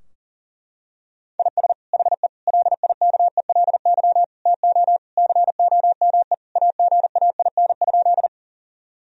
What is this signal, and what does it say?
Morse code, heard as a keyed tone, about 40 wpm, 700 Hz. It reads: IS HE LIKELY TO COME AGAIN?
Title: Sentences from Top 1000 Words - Review Part 4 - 40wpm